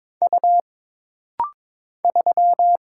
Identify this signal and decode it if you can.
Morse code, heard as a keyed tone, about 22 wpm, 700 Hz. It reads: U 3
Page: This rapid-fire timing is geared to push your ICR (Instant Character Recognition) skills.